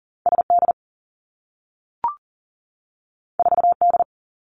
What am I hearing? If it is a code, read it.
Morse code, heard as a keyed tone, about 39 wpm, 700 Hz. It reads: SD 4D
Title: Two Challenging Letter-Number Combinations 40wpm